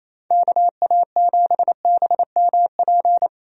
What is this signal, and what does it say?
Morse code, heard as a keyed tone, about 27 wpm, 700 Hz. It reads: KA7BMP
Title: USA Call Signs - 6 Characters - 28wpm